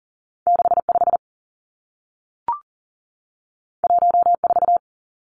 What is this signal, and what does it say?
Morse code, heard as a keyed tone, about 42 wpm, 700 Hz. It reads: 6 5 1 4
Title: Two Challenging Letter-Number Combinations 40wpm